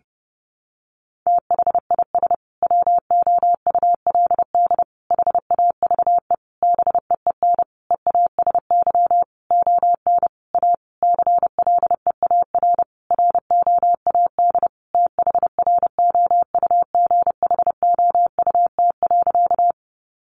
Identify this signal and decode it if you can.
Morse code, heard as a keyed tone, about 30 wpm, 700 Hz. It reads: THIS WOULD HAVE BEEN EASY ON A CLEAR ROAD THROUGHOUT.
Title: Sentences from Top 800 Words 30wpm